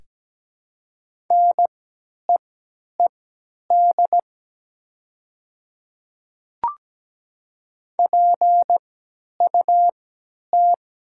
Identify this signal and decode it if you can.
Morse code, heard as a keyed tone, about 17 wpm, spaced as for 10 wpm, 700 Hz. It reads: NEED PUT
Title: 3x Character Spacing - Top 200 Words - 17wpm